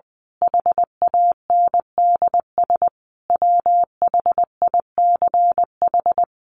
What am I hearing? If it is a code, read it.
Morse code, heard as a keyed tone, about 20 wpm, 700 Hz. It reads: HANDS WHICH
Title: A Sound Of Thunder 20wpm